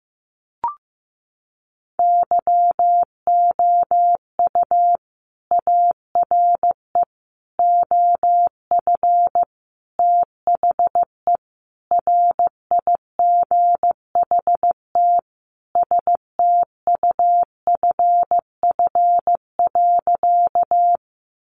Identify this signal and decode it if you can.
Morse code, heard as a keyed tone, about 15 wpm, 700 Hz. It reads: YOU ARE OF THE RIGHT STUFF.
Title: Sentences from Top 1400 Words 15wpm